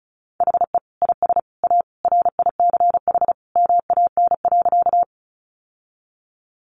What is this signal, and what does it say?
Morse code, heard as a keyed tone, about 34 wpm, 700 Hz. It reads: HE IS A RICH MAN.
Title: Sentences from Top 1100 Words 35wpm